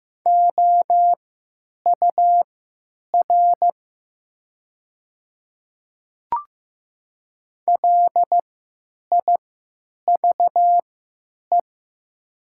Morse code, heard as a keyed tone, about 15 wpm, spaced as for 9 wpm, 700 Hz. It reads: OUR LIVE
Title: 3x Character Spacing - Top 200 Words - 15wpm